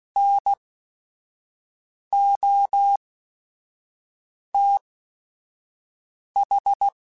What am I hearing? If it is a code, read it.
Morse code, heard as a keyed tone, about 16 wpm, spaced as for 5 wpm, 800 Hz. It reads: NOTH
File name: QSOs / QSOs-mp3s / WA7QOX 1@5